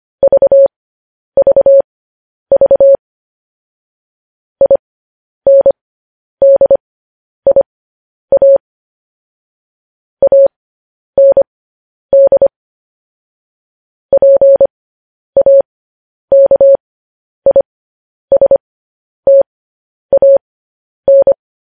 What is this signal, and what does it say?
Morse code, heard as a keyed tone, about 25 wpm, spaced as for 10 wpm, 550 Hz. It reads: VVV INDIA AND PAKISTAN
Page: News Headlines in Morse Code at 10 WPM